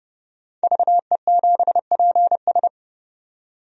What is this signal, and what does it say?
Morse code, heard as a keyed tone, about 29 wpm, 700 Hz. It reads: VE7PS
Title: DXCC-100 Call Signs - 5 Characters - 30wpm